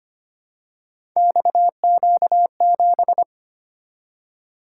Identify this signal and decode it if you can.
Morse code, heard as a keyed tone, about 25 wpm, 700 Hz. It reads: XQ7
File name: Three Letter-Number Combinations - 25wpm